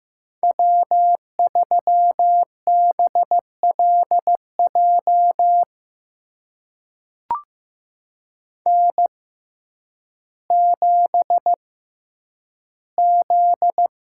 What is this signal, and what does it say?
Morse code, heard as a keyed tone, about 15 wpm, spaced as for 9 wpm, 700 Hz. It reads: W3BLJ N7Z
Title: USA Call Signs - 6x Character Spacing - 15wpm